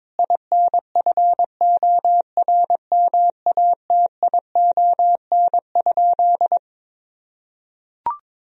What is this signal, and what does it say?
Morse code, heard as a keyed tone, about 22 wpm, 700 Hz. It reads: INFORMATION?
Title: Sentences from Top 1000 Words - Review Part 3 - 22wpm